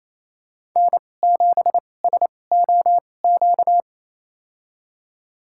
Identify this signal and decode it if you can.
Morse code, heard as a keyed tone, about 27 wpm, spaced as for 20 wpm, 700 Hz. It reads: N7SOQ
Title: USA Call Signs - 5 Characters - 2x Character Spacing - 28wpm